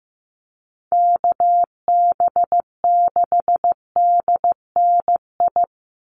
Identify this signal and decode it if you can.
Morse code, heard as a keyed tone, about 15 wpm, 700 Hz. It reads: KB6DNI
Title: USA Call Signs 15wpm